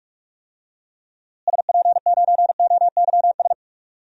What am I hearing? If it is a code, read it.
Morse code, heard as a keyed tone, about 43 wpm, 700 Hz. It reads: SP9OYS
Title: DXCC-100 Call Signs - 6 Characters - Repeated 2x - 45wpm